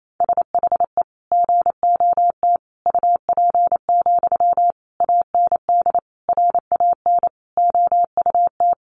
Morse code, heard as a keyed tone, about 28 wpm, 700 Hz. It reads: SHE GOT UP, AND RAN OUT
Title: Sentences from Top 500 Words 28wpm